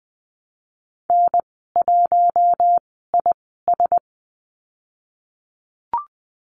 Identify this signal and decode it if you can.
Morse code, heard as a keyed tone, about 20 wpm, spaced as for 15 wpm, 700 Hz. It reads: N1IS
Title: 048 Lesson 9g - Number 4 - Call Signs 20wpm